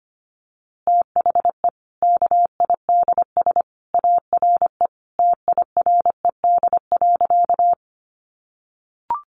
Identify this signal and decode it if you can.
Morse code, heard as a keyed tone, about 25 wpm, 700 Hz. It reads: THE KIDS ARE TIRED.
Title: Easy Phrases 25wpm